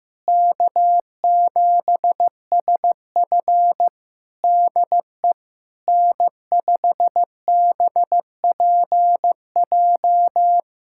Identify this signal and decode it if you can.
Morse code, heard as a keyed tone, about 15 wpm, 700 Hz. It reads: K7SF DE N5BPJ
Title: Practice QSO - Exchange 4 - Ending - 15wpm